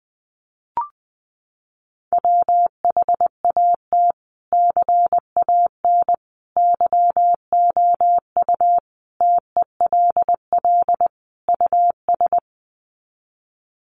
Morse code, heard as a keyed tone, about 20 wpm, 700 Hz. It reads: WHAT CAN YOU TELL US
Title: Sets of 5 Words Encore 20wpm